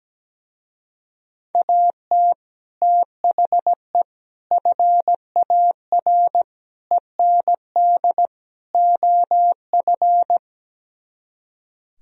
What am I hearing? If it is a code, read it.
Morse code, heard as a keyed tone, about 17 wpm, 700 Hz. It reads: AT THE FAR END OF
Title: Speed-Racing - Sets of 5 Words 17wpm